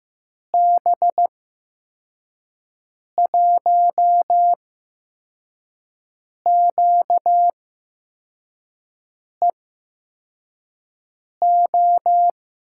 Morse code, heard as a keyed tone, about 15 wpm, spaced as for 4 wpm, 700 Hz. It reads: B1QEO
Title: USA Call Signs - 6 Characters - 8x Character Spacing - 15wpm